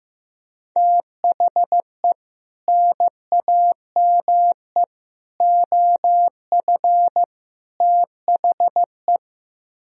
Morse code, heard as a keyed tone, about 15 wpm, 700 Hz. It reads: THE NAME OF THE
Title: Speed-Racing - Sets of 4 Words 15wpm